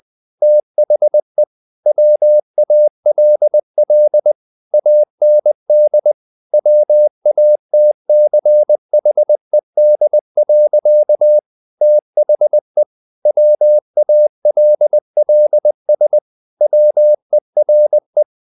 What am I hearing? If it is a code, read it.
Morse code, heard as a keyed tone, about 20 wpm, 600 Hz. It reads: THE WALL AND WATCHED. THE WALLS WERE
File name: the old man and the sea part 2 20wpm